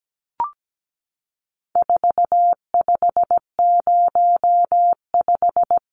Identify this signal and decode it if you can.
Morse code, heard as a keyed tone, about 17 wpm, 700 Hz. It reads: 4505
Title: Numbers - 4-Digit 17wpm